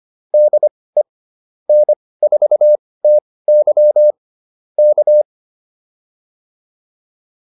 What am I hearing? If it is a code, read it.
Morse code, heard as a keyed tone, about 25 wpm, spaced as for 18 wpm, 600 Hz. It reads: DE N4TY K